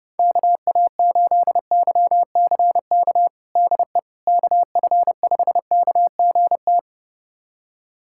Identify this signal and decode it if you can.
Morse code, heard as a keyed tone, about 30 wpm, 700 Hz. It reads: KA8YCK DE KF5KGT
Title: USA Callsign DE Callsign - 6 Characters - 30wpm